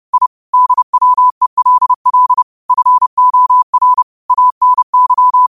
Morse code, heard as a keyed tone, about 30 wpm, 1000 Hz. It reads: I DWELL FOR ANY
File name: qotdfiles / 30wpm